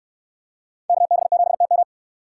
Morse code, heard as a keyed tone, about 69 wpm, 700 Hz. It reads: KB7TM